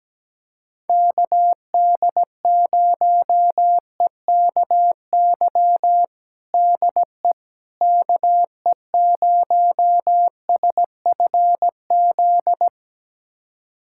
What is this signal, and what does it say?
Morse code, heard as a keyed tone, about 17 wpm, 700 Hz. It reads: KD0EKY DE KE0SFZ
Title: USA Callsign DE Callsign - 6 Characters - 17wpm